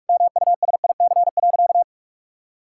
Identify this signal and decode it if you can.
Morse code, heard as a keyed tone, about 46 wpm, 700 Hz. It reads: MUSIC.
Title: Sentences from Top 1600 Words 45wpm